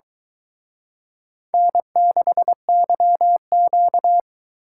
Morse code, heard as a keyed tone, about 23 wpm, 700 Hz. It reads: N6YQ